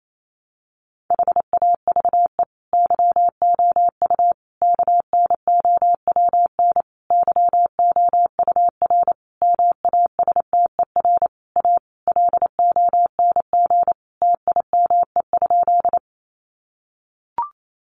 Morse code, heard as a keyed tone, about 28 wpm, 700 Hz. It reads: HAVE YOU KNOWN YOUR MASTER A LONG TIME?
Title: Sentences from Top 1300 Words 28wpm